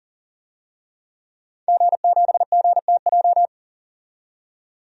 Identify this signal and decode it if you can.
Morse code, heard as a keyed tone, about 40 wpm, 700 Hz. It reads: G7GTJ
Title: DXCC-100 Call Signs - 5 Characters - 40wpm